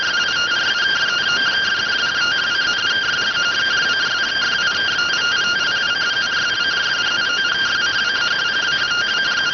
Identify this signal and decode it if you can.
Two-tone teleprinter signal, about 170 Hz shift.